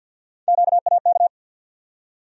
Morse code, heard as a keyed tone, about 47 wpm, 700 Hz. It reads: QAK
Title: USA Call Sign Postfixes - 3 Letters 50wpm